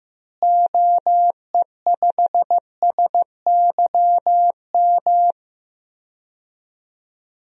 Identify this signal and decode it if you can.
Morse code, heard as a keyed tone, about 15 wpm, 700 Hz. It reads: OE5SYM
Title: DXCC-100 Call Signs - 6 Characters - Repeated 2x - 15wpm